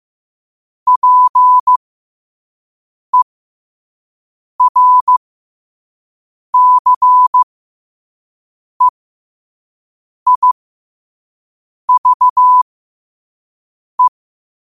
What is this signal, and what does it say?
Morse code, heard as a keyed tone, about 15 wpm, spaced as for 5 wpm, 1000 Hz. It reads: PERCEIVE
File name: qotdfiles / 5wpm